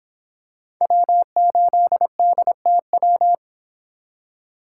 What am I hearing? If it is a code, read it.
Morse code, heard as a keyed tone, about 25 wpm, 700 Hz. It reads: W8DTW